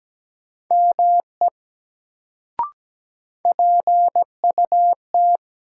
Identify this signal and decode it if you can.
Morse code, heard as a keyed tone, about 17 wpm, 700 Hz. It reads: ME PUT
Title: Sentences from Top 200 Words - New Word Drill - 17wpm